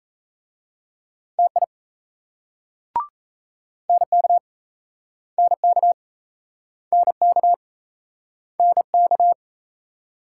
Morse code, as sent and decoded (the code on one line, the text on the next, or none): - .. / -. -.- / -. -.- / -. -.- / -. -.-
TI NK NK NK NK